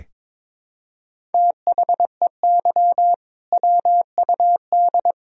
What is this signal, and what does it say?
Morse code, heard as a keyed tone, about 22 wpm, 700 Hz. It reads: THEY WUD
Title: Sentences from Top 500 Words and Abbreviations 22wpm